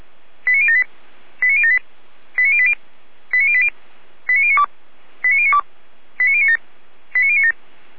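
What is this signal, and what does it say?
Five-tone selective call (ZVEI), heard as quick runs of five tones, tones of about 70 ms. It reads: 89078 89078 89080 89080 89092 89092 89097 89097
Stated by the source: ZVEI five-tone selective call received via SDR.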